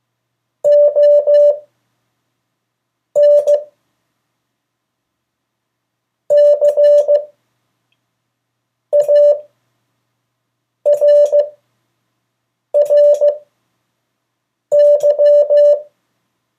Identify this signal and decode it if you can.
Morse code, heard as a keyed tone, about 15 wpm, spaced as for 5 wpm, 580 Hz. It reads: ON CARRY
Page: CW Class Review – Delaware Valley Radio Association